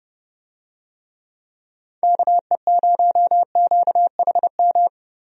Morse code, heard as a keyed tone, about 32 wpm, 700 Hz. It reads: KE0QHM
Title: USA Call Signs - 6 Characters - Repeated 2x - 30wpm